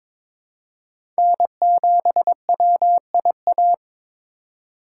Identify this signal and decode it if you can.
Morse code, heard as a keyed tone, about 22 wpm, 700 Hz. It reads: N7WIA
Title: USA Call Signs - 5 Characters - 3x Character Spacing - 22wpm